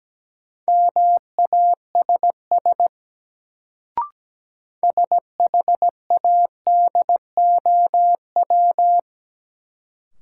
Morse code, heard as a keyed tone, about 17 wpm, 700 Hz. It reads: MASS SHADOW
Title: Sets of 5 Words Encore - New Word Drill - 17wpm